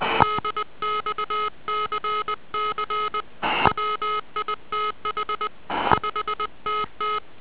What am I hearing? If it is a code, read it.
Morse code, heard as a keyed tone, about 20 wpm, 1270 Hz. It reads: DXCC WITH 5TT